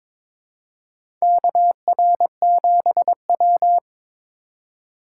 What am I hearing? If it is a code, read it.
Morse code, heard as a keyed tone, about 22 wpm, 700 Hz. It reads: KR7W
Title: K1USN SST QSO Practice - 22wpm